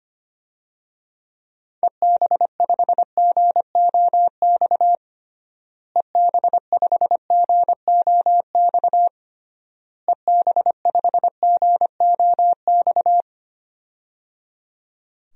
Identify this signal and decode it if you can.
Morse code, heard as a keyed tone, about 25 wpm, 700 Hz. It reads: EB5GOX EB5GOX EB5GOX
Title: DXCC-100 Call Signs - 6 Characters - Repeated 3x - 25wpm